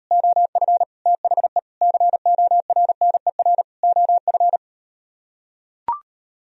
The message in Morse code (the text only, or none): OF THE CORNER OF